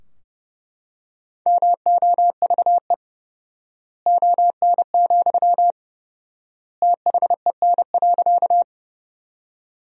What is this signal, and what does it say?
Morse code, heard as a keyed tone, about 30 wpm, 700 Hz. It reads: MOVE ON, THEN.
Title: 4x Word-Spacing - Sentences from Top 200 Words - 30wpm